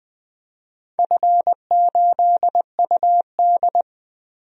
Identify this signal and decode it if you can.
Morse code, heard as a keyed tone, about 20 wpm, 700 Hz. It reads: F8UD
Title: DXCC-100 Call Signs - 4 Characters - 20wpm